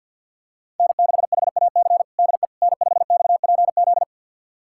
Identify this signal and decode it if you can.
Morse code, heard as a keyed tone, about 52 wpm, 700 Hz. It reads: N6HAC DE N5XPZ
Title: USA Callsign DE Callsign - 5 Characters - 50wpm